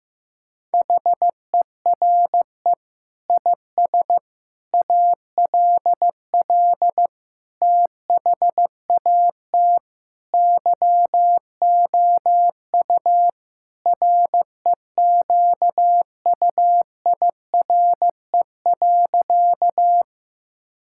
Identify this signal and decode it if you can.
Morse code, heard as a keyed tone, about 16 wpm, 700 Hz. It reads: HERE IS ALL THAT YOU REQUIRE.
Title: Sentences from Top 1000 Words - Review Part 6 - 15wpm